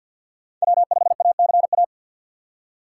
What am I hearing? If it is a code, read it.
Morse code, heard as a keyed tone, about 47 wpm, 700 Hz. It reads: W5ACA
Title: USA Call Signs - 5 Characters - 3x Character Spacing - 50wpm